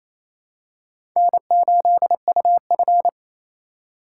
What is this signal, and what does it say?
Morse code, heard as a keyed tone, about 28 wpm, 700 Hz. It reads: N8UF
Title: USA Call Signs - 4 Characters - 4x Character Spacing - 28wpm